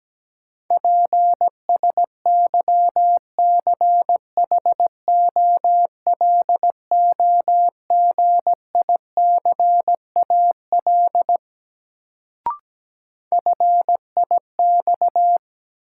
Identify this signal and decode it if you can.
Morse code, heard as a keyed tone, about 17 wpm, 700 Hz. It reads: PSYCHOLOGICAL FIX
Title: Sentences from Top 1400 Words - New Word Drill - 17wpm